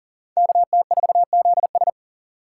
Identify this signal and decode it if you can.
Morse code, heard as a keyed tone, about 51 wpm, 700 Hz. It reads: KT4ZS